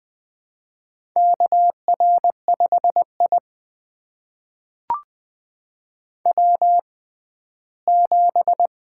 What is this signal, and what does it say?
Morse code, heard as a keyed tone, about 20 wpm, spaced as for 14 wpm, 700 Hz. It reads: KR5I W7
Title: USA Call Signs - 4 Characters - 6x Character Spacing - 20wpm